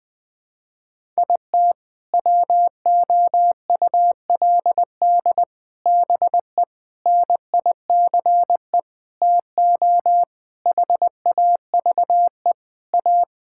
Morse code, heard as a keyed tone, about 20 wpm, 700 Hz. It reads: IT WOULD BE NICE TO HAVE A
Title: Sentences with 7-Word Phrases - 20wpm